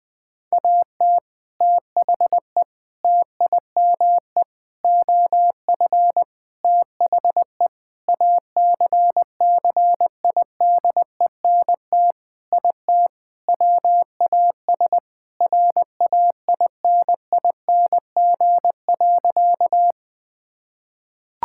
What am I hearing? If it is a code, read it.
Morse code, heard as a keyed tone, about 20 wpm, 700 Hz. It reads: AT THE TIME OF THE ACCIDENT IT WAS RAINING.
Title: Sentences with 5-Word Phrases - 20wpm